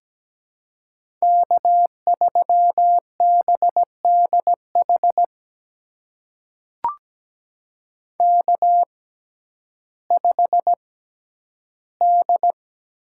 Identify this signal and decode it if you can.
Morse code, heard as a keyed tone, about 17 wpm, spaced as for 10 wpm, 700 Hz. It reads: K3BDH K5D
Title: USA Call Signs - 5 Characters - 6x Character Spacing - 17wpm